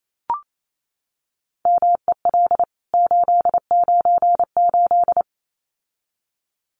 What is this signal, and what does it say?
Morse code, heard as a keyed tone, about 28 wpm, 700 Hz. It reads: MEL 898
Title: CWT Exchange 28wpm